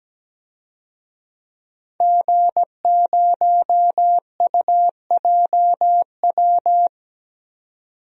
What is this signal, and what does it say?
Morse code, heard as a keyed tone, about 17 wpm, 700 Hz. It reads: G0UJW
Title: DXCC-100 Call Signs - 5 Characters - 17wpm